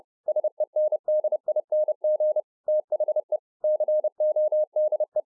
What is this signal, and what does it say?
Morse code, heard as a keyed tone, about 30 wpm, 610 Hz. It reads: SENDING THE CODE